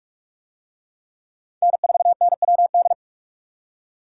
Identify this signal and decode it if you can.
Morse code, heard as a keyed tone, about 44 wpm, 700 Hz. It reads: N4NWD